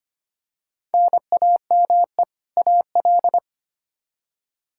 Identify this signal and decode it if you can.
Morse code, heard as a keyed tone, about 25 wpm, 700 Hz. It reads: NAME AL
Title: K1USN SST QSO Practice 25wpm